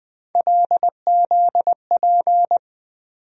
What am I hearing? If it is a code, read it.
Morse code, heard as a keyed tone, about 21 wpm, 700 Hz. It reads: LZP
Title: USA Call Sign Postfixes - 3 Letters 20wpm